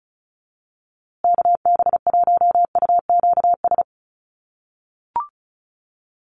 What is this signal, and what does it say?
Morse code, heard as a keyed tone, about 34 wpm, 700 Hz. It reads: KB1UQS
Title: USA Call Signs Repeated 3x 35wpm